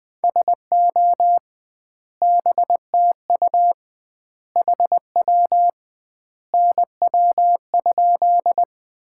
Morse code, heard as a keyed tone, about 21 wpm, 700 Hz. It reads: SO BTU HW NW?
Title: Common QSO Phrases - 2x Word-Spacing - 20wpm